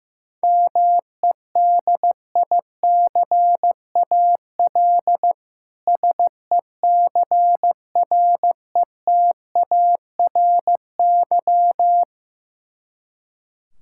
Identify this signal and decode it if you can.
Morse code, heard as a keyed tone, about 15 wpm, 700 Hz. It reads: MEDICAL SECRETARY
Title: K1USN SST QSO Practice 15wpm